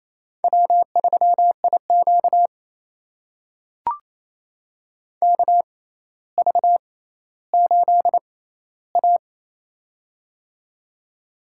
Morse code, heard as a keyed tone, about 28 wpm, spaced as for 14 wpm, 700 Hz. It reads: W3IQ KV8A
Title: USA Call Signs - 4 Characters - 6x Character Spacing - 28wpm